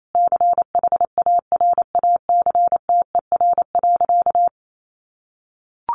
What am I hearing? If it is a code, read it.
Morse code, heard as a keyed tone, about 28 wpm, 700 Hz. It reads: CHARACTER.
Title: Sentences from Top 1200 Words 28wpm